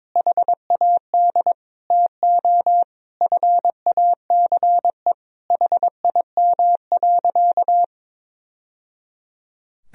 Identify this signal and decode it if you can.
Morse code, heard as a keyed tone, about 22 wpm, 700 Hz. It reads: HAD TO FACE HIM.
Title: Sentences from Top 1000 Words - Review Part 2 - 22wpm